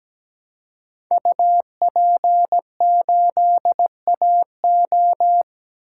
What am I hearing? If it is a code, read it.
Morse code, heard as a keyed tone, about 17 wpm, 700 Hz. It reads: UP8AO
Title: Code Groups - 5 Letter:Numbers - 17wpm